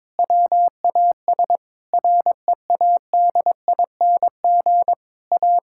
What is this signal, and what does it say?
Morse code, heard as a keyed tone, about 21 wpm, 700 Hz. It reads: WAS READING A
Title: Sentences from Top 1000 Words - Review Part 6 - 22wpm